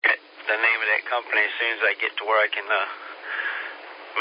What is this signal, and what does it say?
Emergency Alert System (SAME) tones, heard as harsh buzzing bursts, with speech. It is heard from 0.4 to 3.7 s.